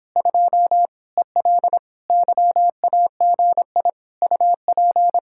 Morse code, heard as a keyed tone, about 26 wpm, 700 Hz. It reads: 2 EL YAGI UP